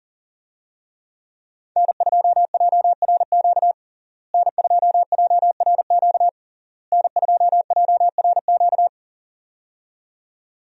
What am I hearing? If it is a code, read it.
Morse code, heard as a keyed tone, about 40 wpm, 700 Hz. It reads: N2JRQ N2JRQ N2JRQ
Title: USA Call Signs - 5 Characters - Repeated 3x - 40wpm